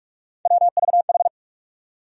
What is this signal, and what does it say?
Morse code, heard as a keyed tone, about 41 wpm, 700 Hz. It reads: WVH
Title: USA Call Sign Postfixes - 3 Letters 45wpm